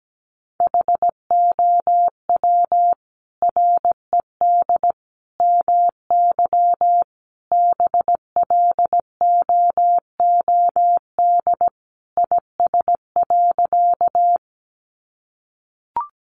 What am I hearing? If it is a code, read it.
Morse code, heard as a keyed tone, about 17 wpm, 700 Hz. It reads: HOW RED MY BLOOD IS.
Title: Sentences from Top 1100 Words 17wpm